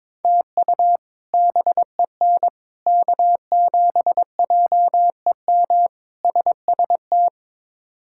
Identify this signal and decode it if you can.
Morse code, heard as a keyed tone, about 22 wpm, 700 Hz. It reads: TU BEN K7JEM SST
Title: K1USN SST QSO Practice - 22wpm